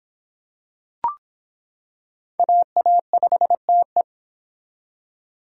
Morse code, heard as a keyed tone, about 26 wpm, 700 Hz. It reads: AA5TE